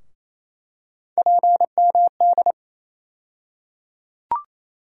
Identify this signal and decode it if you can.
Morse code, heard as a keyed tone, about 27 wpm, 700 Hz. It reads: PMD
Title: USA Call Sign Postfixes - 3 Letters 28wpm